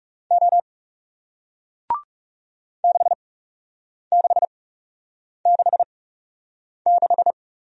Morse code, heard as a keyed tone, about 40 wpm, spaced as for 8 wpm, 700 Hz. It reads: O 6666